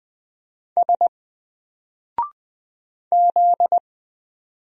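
Morse code, heard as a keyed tone, about 20 wpm, 700 Hz. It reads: S Z
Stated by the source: Single letter and number Morse drill at 20 wpm, from N4LCD.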